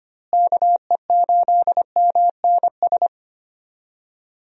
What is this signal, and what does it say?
Morse code, heard as a keyed tone, about 25 wpm, 700 Hz. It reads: KE8MNS
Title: USA Call Signs - 6 Characters - Repeated 3x - 25wpm